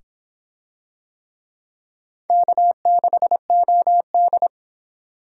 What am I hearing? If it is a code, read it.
Morse code, heard as a keyed tone, about 26 wpm, 700 Hz. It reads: K6OD